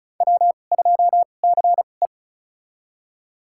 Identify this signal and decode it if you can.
Morse code, heard as a keyed tone, about 36 wpm, spaced as for 25 wpm, 700 Hz. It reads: W2CE
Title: USA Call Signs - 4 Characters - 2x Character Spacing - 35wpm